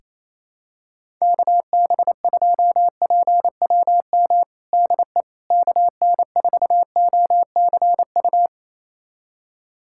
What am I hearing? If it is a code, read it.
Morse code, heard as a keyed tone, about 28 wpm, 700 Hz. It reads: KB2PWM DE KN4OCU
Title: USA Callsign DE Callsign - 6 Characters - 28wpm